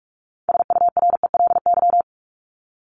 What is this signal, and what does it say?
Morse code, heard as a keyed tone, about 44 wpm, 700 Hz. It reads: SURELY
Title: LY Suffix - Top 100 Words with Base Words 45wpm